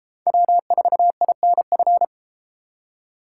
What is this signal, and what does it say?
Morse code, heard as a keyed tone, about 30 wpm, 700 Hz. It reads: W4INF